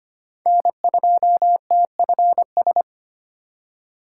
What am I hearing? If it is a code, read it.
Morse code, heard as a keyed tone, about 25 wpm, 700 Hz. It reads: N2TFS
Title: USA Call Signs - 5 Characters - 3x Character Spacing - 25wpm